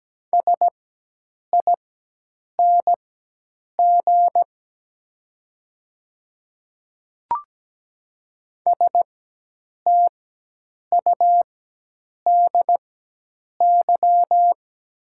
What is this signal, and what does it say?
Morse code, heard as a keyed tone, about 17 wpm, spaced as for 8 wpm, 700 Hz. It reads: SING STUDY
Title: 4x Character Spacing - Top 500 Words - 17wpm